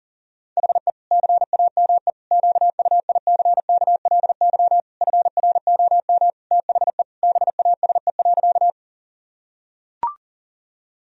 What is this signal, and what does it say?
Morse code, heard as a keyed tone, about 40 wpm, 700 Hz. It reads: HE CAME QUICKLY FROM THE BASE.
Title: Sentences from Top 1000 Words - Review Part 1 - 40wpm